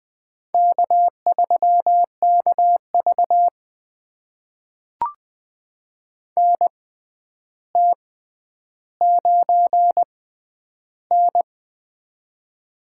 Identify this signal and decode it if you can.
Morse code, heard as a keyed tone, about 20 wpm, spaced as for 10 wpm, 700 Hz. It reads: K3KV NT9N
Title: USA Call Signs - 4 Characters - 6x Character Spacing - 20wpm